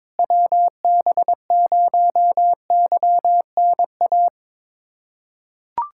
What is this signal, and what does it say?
Morse code, heard as a keyed tone, about 22 wpm, 700 Hz. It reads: WB0YNA
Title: USA Callsign DE Callsign - 6 Characters - 22wpm